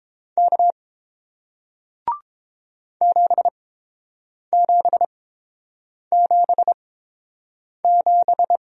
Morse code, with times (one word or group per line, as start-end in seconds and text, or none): 0.4-0.7 K
3.0-8.6 7777